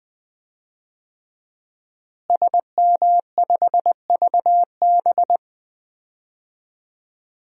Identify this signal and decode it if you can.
Morse code, heard as a keyed tone, about 21 wpm, 700 Hz. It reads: SM5VB
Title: DXCC-100 Call Signs - 5 Characters - Repeated 3x - 20wpm